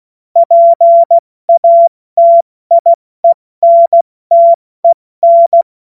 Morse code, heard as a keyed tone, about 16 wpm, 680 Hz. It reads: PATIENTEN
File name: Morsekurs / Witze / Witz 25. Juli 2023 70 F80